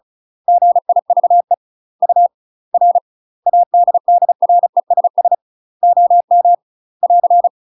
Morse code, heard as a keyed tone, about 36 wpm, spaced as for 31 wpm, 700 Hz. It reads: GIVE UR ADDRESS OM <AR>
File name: UK MASTER QSOs TRAINING / 35WPM / UK MASTER QSOs 0033